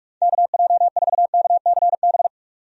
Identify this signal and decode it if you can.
Morse code, heard as a keyed tone, about 46 wpm, 700 Hz. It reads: KJ4KCB
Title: USA Call Signs - 6 Characters - 45wpm